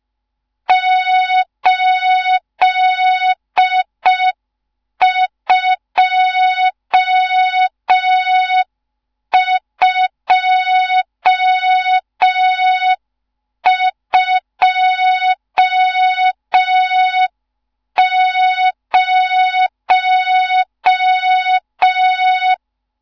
Morse code, heard as a keyed tone, about 5 wpm, 3700 Hz. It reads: 82220
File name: Morze-HA2MN / hang / 14 280 HA2MN